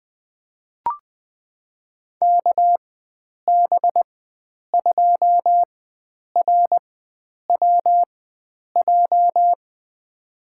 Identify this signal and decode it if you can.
Morse code, heard as a keyed tone, about 20 wpm, spaced as for 9 wpm, 700 Hz. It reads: KB2RWJ
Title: USA Call Signs - 6 Characters - 4x Character Spacing - 20wpm